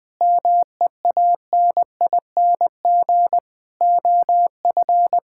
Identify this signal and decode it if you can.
Morse code, heard as a keyed tone, about 20 wpm, 700 Hz. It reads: MEANING OF
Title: Sentences from Top 1000 Words - Review Part 6 - 20wpm